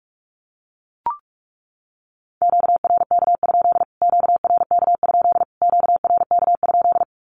Morse code, heard as a keyed tone, about 44 wpm, 700 Hz. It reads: QRK? QRK? QRK?
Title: Q-Codes Repeated 3x 45wpm